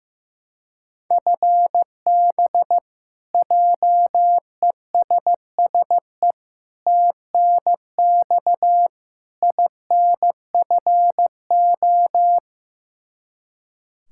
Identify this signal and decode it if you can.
Morse code, heard as a keyed tone, about 16 wpm, 700 Hz. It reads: FB JESSE TNX INFO
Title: K1USN SST QSO Practice 15wpm